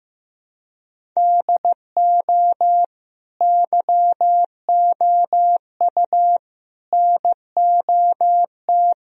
Morse code, heard as a keyed tone, about 16 wpm, 700 Hz. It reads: DO YOU NOT
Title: Sentences from Top 1000 Words - Review Part 3 - 15wpm